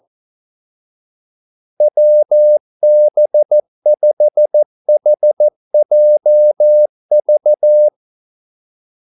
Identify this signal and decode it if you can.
Morse code, heard as a keyed tone, about 14 wpm, 600 Hz. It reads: WB5HJV
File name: Callsigns-14wpm